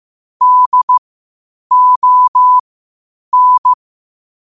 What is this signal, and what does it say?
Morse code, heard as a keyed tone, about 15 wpm, spaced as for 8 wpm, 1000 Hz. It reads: DON
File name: qotdfiles / 8wpm